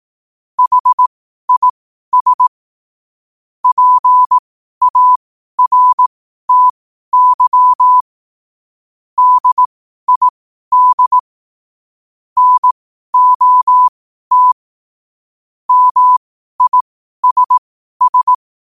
Morse code, heard as a keyed tone, about 18 wpm, spaced as for 12 wpm, 1000 Hz. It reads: HIS PARTY DID NOT MISS
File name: 12wpm